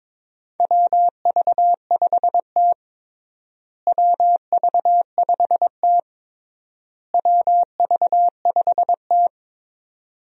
Morse code, heard as a keyed tone, about 22 wpm, 700 Hz. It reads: WV5T WV5T WV5T
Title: USA Call Signs - 4 Characters - Repeated 3x - 22wpm